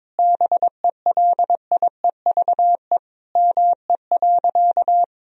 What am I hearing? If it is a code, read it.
Morse code, heard as a keyed tone, about 22 wpm, 700 Hz. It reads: BELIEVE ME.
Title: Sentences from Top 1000 Words - Review Part 3 - 22wpm